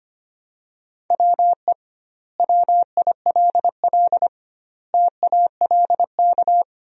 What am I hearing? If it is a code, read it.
Morse code, heard as a keyed tone, about 25 wpm, 700 Hz. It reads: WE WILL TALK
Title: 2x Word-Spacing - Sentences from Top 400 Words - 25wpm